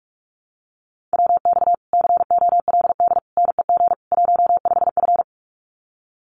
Morse code, heard as a keyed tone, about 44 wpm, 700 Hz. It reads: WX COLD NEG 15F
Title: Common QSO Phrases Encore 45wpm